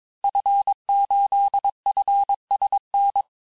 Morse code, heard as a keyed tone, about 22 wpm, 800 Hz. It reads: F8FSN
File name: cours-du-mercredi / audios / cours-854-liste-fin-QSO-vitesse-21-mpm